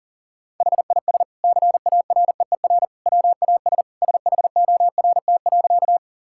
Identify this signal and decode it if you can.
Morse code, heard as a keyed tone, about 40 wpm, 700 Hz. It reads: HIS CAREER WAS SHORT.
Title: Sentences from Top 1000 Words - Review Part 6 - 40wpm